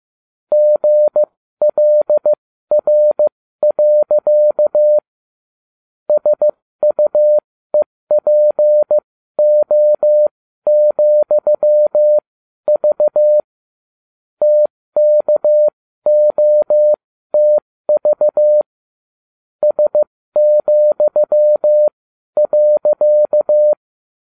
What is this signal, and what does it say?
Morse code, heard as a keyed tone, about 15 wpm, spaced as for 12 wpm, 600 Hz. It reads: GLR. SUEPO,V TKOTV S,.